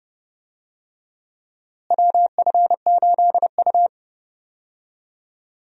Morse code, heard as a keyed tone, about 31 wpm, 700 Hz. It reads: WF8U
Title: USA Call Signs - 4 Characters - Repeated 2x - 30wpm